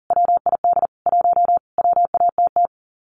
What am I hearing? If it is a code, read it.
Morse code, heard as a keyed tone, about 40 wpm, 700 Hz. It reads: WID 1 WATT
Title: Common QSO Phrases 40wpm